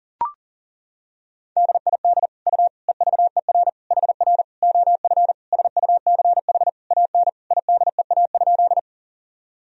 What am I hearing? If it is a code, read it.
Morse code, heard as a keyed tone, about 37 wpm, 700 Hz. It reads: DID U EVER HR OF SUCH AN IDEA?
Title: Sentences from Top 400 Words and Abbreviations 40wpm